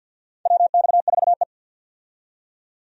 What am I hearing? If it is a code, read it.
Morse code, heard as a keyed tone, about 53 wpm, 700 Hz. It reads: WX4E
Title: USA Call Signs - 4 Characters - 50wpm